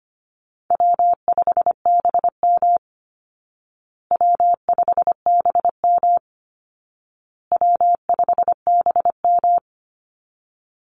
Morse code, heard as a keyed tone, about 25 wpm, 700 Hz. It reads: W5BM W5BM W5BM
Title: Easy Call Signs Repeated 3x - 25wpm